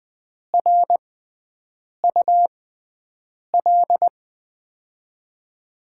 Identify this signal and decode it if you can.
Morse code, heard as a keyed tone, about 20 wpm, spaced as for 7 wpm, 700 Hz. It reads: RUL
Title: USA Call Signs - 6 Characters - 6x Character Spacing - 20wpm